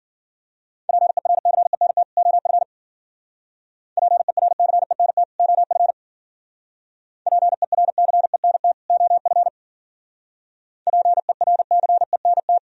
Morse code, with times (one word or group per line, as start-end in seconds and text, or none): 0.9-2.0 PERCENT
2.2-2.6 OF
4.0-5.2 PERCENT
5.4-5.9 OF
7.3-8.7 PERCENT
8.9-9.5 OF
10.9-12.6 PERCENT